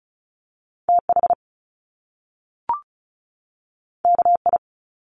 Morse code, heard as a keyed tone, about 36 wpm, 700 Hz. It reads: TH KI
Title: Top Two Letter Combinations in Words 35wpm